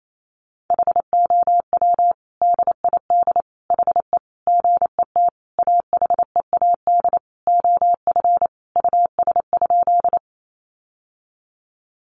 Morse code, heard as a keyed tone, about 28 wpm, 700 Hz. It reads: HOW DID HE GET AHEAD OF US?
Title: Sentences from Top 1100 Words 28wpm